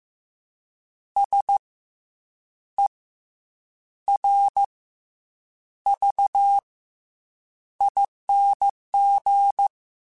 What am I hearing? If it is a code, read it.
Morse code, heard as a keyed tone, about 15 wpm, 800 Hz. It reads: S E R V ING
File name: Special INGs / Special INGs-mp3s / SlowToFaster ing @13